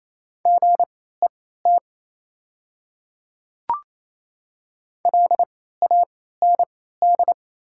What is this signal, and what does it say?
Morse code, heard as a keyed tone, about 28 wpm, spaced as for 16 wpm, 700 Hz. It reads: GET LAND
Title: 3x Character Spacing - Top 200 Words - 28wpm